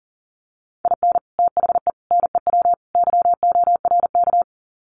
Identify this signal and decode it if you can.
Morse code, heard as a keyed tone, about 40 wpm, 700 Hz. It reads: IN THE NEW YORK
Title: Sets of 4 Words 40wpm